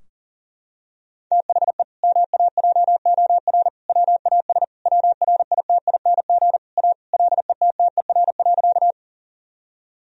Morse code, heard as a keyed tone, about 37 wpm, 700 Hz. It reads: THE MAJOR WAS WRITING A LETTER.
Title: Sentences from Top 1000 Words - Review Part 4 - 40wpm